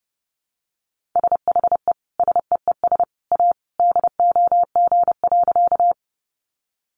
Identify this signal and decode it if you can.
Morse code, heard as a keyed tone, about 30 wpm, 700 Hz. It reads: SHE SEES A DOG.
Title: Easy Phrases 30wpm